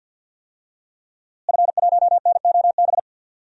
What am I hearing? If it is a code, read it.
Morse code, heard as a keyed tone, about 47 wpm, 700 Hz. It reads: F1NOB
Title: DXCC-100 Call Signs - 5 Characters - Repeated 3x - 50wpm